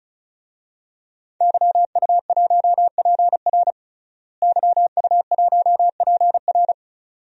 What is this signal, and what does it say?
Morse code, heard as a keyed tone, about 34 wpm, 700 Hz. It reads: YU1PR YU1PR
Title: DXCC-100 Call Signs - 5 Characters - Repeated 2x - 35wpm